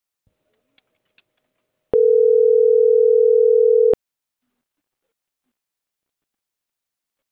Ringback tone (North America), 440 + 480 Hz.